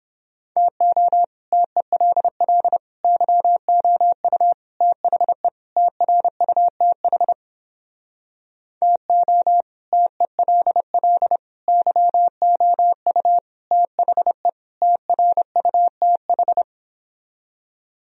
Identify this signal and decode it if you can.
Morse code, heard as a keyed tone, about 29 wpm, 700 Hz. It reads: TO TELL YOU THE TRUTH TO TELL YOU THE TRUTH